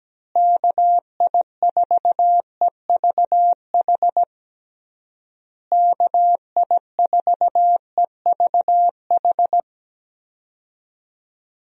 Morse code, heard as a keyed tone, about 17 wpm, 700 Hz. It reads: KI4EVH KI4EVH
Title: USA Call Signs - 6 Characters - Repeated 3x - 17wpm